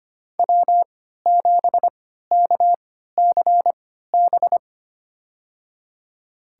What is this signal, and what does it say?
Morse code, heard as a keyed tone, about 25 wpm, spaced as for 14 wpm, 700 Hz. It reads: W7KCB
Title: USA Call Signs - 3x Character Spacing - 25wpm